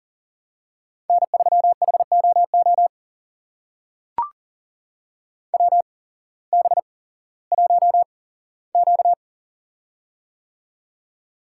Morse code, heard as a keyed tone, about 38 wpm, spaced as for 18 wpm, 700 Hz. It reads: N3HOO WB1Q
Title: USA Call Signs - 8x Character Spacing - 40wpm